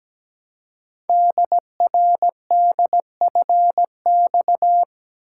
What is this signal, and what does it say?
Morse code, heard as a keyed tone, about 17 wpm, 700 Hz. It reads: DRDFX
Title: Code Groups - 5 Letters - 17wpm